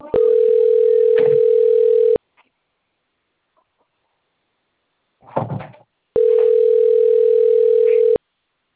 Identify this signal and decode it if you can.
Ringback tone (North America), 440 + 480 Hz.